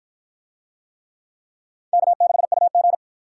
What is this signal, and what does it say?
Morse code, heard as a keyed tone, about 50 wpm, 700 Hz. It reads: K6UG